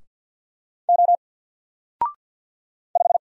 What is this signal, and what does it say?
Morse code, heard as a keyed tone, about 48 wpm, 700 Hz. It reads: O 5
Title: Single Letter-Number Alt 50wpm